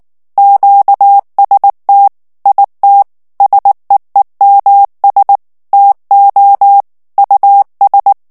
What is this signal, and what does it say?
Morse code, heard as a keyed tone, about 19 wpm, 800 Hz. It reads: QST IT SEEMS TO US
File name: ANGLAIS texte clair 3 vitesse 18 mots minute